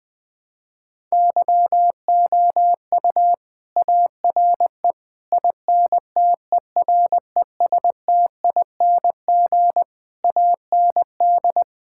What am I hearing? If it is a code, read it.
Morse code, heard as a keyed tone, about 20 wpm, 700 Hz. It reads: YOU ARE INTERESTING AND
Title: Sentences from Top 1000 Words - Review Part 6 - 20wpm